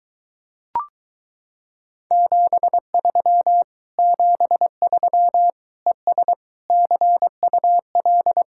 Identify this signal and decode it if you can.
Morse code, heard as a keyed tone, about 22 wpm, 700 Hz. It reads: 73 73 ES CUL